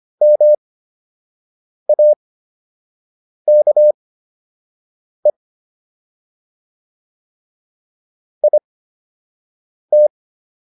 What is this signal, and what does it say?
Morse code, heard as a keyed tone, about 25 wpm, spaced as for 6 wpm, 600 Hz. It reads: MAKE IT